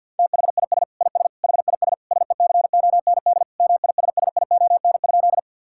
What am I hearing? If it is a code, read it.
Morse code, heard as a keyed tone, about 50 wpm, 700 Hz. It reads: THIS IS HIS SECOND MISSION?
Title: Sentences from Top 1600 Words 50wpm